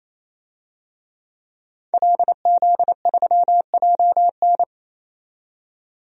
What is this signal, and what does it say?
Morse code, heard as a keyed tone, about 28 wpm, 700 Hz. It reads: LZ3JN
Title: DXCC-100 Call Signs - 5 Characters - 28wpm